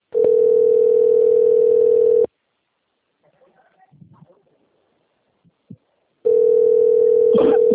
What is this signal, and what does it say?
Ringback tone (North America), 440 + 480 Hz.